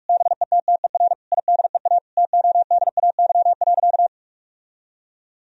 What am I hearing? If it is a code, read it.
Morse code, heard as a keyed tone, about 46 wpm, 700 Hz. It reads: BETTER IDEA TODAY.
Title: Sentences from Top 1000 Words - Review Part 4 - 45wpm